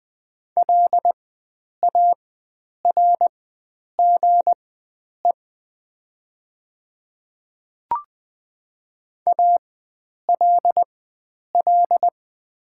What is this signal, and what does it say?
Morse code, heard as a keyed tone, about 20 wpm, spaced as for 9 wpm, 700 Hz. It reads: LARGE ALL
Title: 4x Character Spacing - Top 500 Words - 20wpm